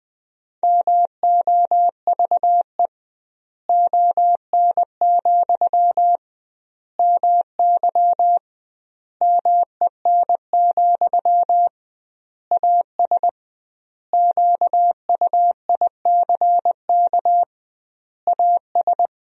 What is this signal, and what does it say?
Morse code, heard as a keyed tone, about 20 wpm, 700 Hz. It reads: MOVE ON, MY MEN, AS QUICK AS
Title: 2x Word-Spacing - Sentences from Top 500 Words - 20wpm